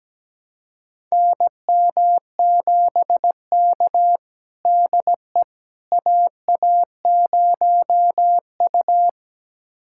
Morse code, heard as a keyed tone, about 17 wpm, 700 Hz. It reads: NM7K DE AA0U
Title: USA Callsign DE Callsign - 4 Characters - 17wpm